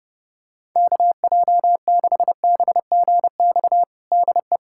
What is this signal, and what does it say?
Morse code, heard as a keyed tone, about 29 wpm, 700 Hz. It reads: KJ6BGX DE
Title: USA Callsign DE Callsign - 6 Characters - 30wpm